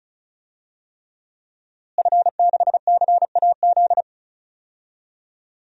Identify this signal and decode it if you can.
Morse code, heard as a keyed tone, about 34 wpm, 700 Hz. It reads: F6CAZ
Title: DXCC-100 Call Signs - 5 Characters - Repeated 3x - 35wpm